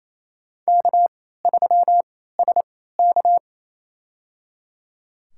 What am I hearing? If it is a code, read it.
Morse code, heard as a keyed tone, about 28 wpm, spaced as for 16 wpm, 700 Hz. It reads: K3SK
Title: USA Call Signs - 4 Characters - 3x Character Spacing - 28wpm